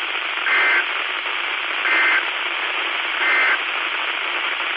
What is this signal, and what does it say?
Emergency Alert System (SAME) tones, heard as harsh buzzing bursts. It is heard from 0.4 to 3.6 s.
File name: Washtenaw County SKYWARN N8DUY and W8SRC Repeaters